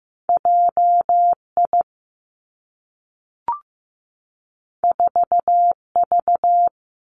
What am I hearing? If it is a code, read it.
Morse code, heard as a keyed tone, about 15 wpm, 700 Hz. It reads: J I 4 V
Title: Two Challenging Letter-Number Combinations 15wpm